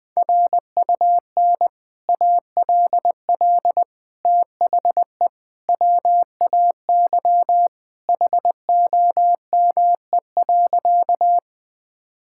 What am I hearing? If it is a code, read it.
Morse code, heard as a keyed tone, about 20 wpm, 700 Hz. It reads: RUN ALL THE WAY HOME.
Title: Sentences with 3-Word Phrases - Spoken Beginning - 20wpm